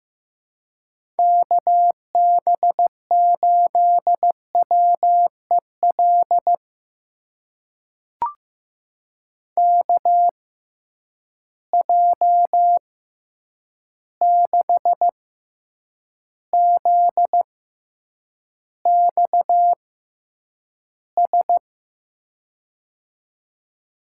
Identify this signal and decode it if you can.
Morse code, heard as a keyed tone, about 15 wpm, spaced as for 8 wpm, 700 Hz. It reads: KB8WEL KJ6ZXS
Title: USA Call Signs - 6 Characters - 6x Character Spacing - 15wpm